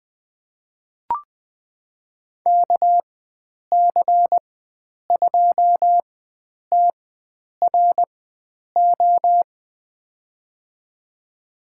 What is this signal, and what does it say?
Morse code, heard as a keyed tone, about 20 wpm, spaced as for 9 wpm, 700 Hz. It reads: KC2TRO
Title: USA Call Signs - 6 Characters - 4x Character Spacing - 20wpm